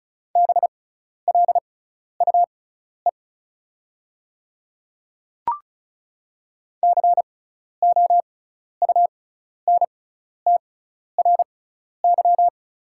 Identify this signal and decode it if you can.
Morse code, heard as a keyed tone, about 34 wpm, spaced as for 12 wpm, 690 Hz. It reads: BLUE COUNTRY
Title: 6x Character Spacing - Top 500 Words - 35wpm